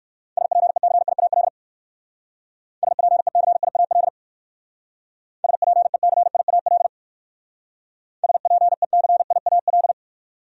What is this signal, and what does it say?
Morse code, heard as a keyed tone, about 54 wpm, 700 Hz. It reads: SPECIAL SPECIAL SPECIAL SPECIAL
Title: Speed-Racing Top 500 Words 45wpm